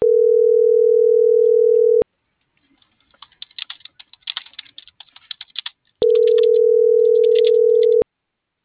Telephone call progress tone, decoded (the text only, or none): ringback tone